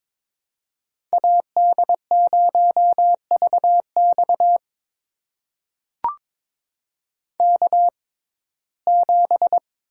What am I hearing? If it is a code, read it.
Morse code, heard as a keyed tone, about 22 wpm, spaced as for 16 wpm, 700 Hz. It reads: AD0VX K7
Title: USA Call Signs - 5 Characters - 6x Character Spacing - 22wpm